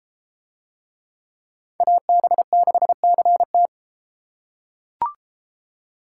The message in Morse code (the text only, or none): AB6CT